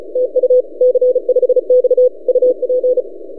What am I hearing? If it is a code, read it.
Morse code, heard as a keyed tone, about 34 wpm, 500 Hz. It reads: TU C5X UP
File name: posts / c5x-30cw- -0600z